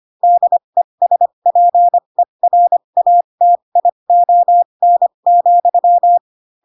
Morse code, heard as a keyed tone, about 25 wpm, spaced as for 22 wpm, 700 Hz. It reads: DESPERATION,
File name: Pinocchio / Pinocchio ch6 18 25 ik1hge